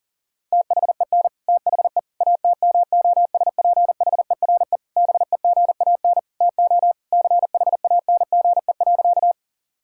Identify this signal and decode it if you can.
Morse code, heard as a keyed tone, about 40 wpm, 700 Hz. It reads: THEN THE ATMOSPHERE BEGAN TO CHANGE.
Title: Sentences from Top 1600 Words 40wpm